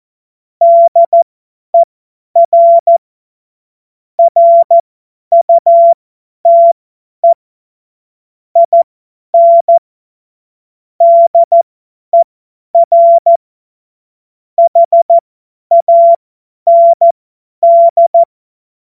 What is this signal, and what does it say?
Morse code, heard as a keyed tone, about 14 wpm, spaced as for 10 wpm, 680 Hz. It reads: DER RUTE IN DER HAND
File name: Morsekurs / Witze / Witz 15. August 2023 50 F70